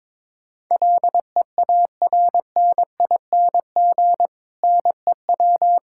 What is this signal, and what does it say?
Morse code, heard as a keyed tone, about 22 wpm, 700 Hz. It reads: LEARNING NEW
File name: Difficult Sentences with Hints - 22wpm